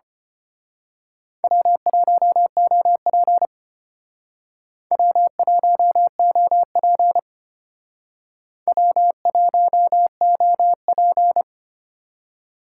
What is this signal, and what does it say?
Morse code, heard as a keyed tone, about 29 wpm, 700 Hz. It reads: W1OP W1OP W1OP
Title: Speed Racing - USA 1x2 Call Signs 25wpm